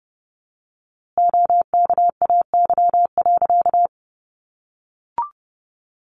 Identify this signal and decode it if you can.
Morse code, heard as a keyed tone, about 30 wpm, 700 Hz. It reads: OKAY.
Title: A Sound Of Thunder 30wpm